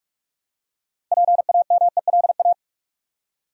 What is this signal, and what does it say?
Morse code, heard as a keyed tone, about 42 wpm, 700 Hz. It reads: PAMELA
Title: USA Common Names - 45wpm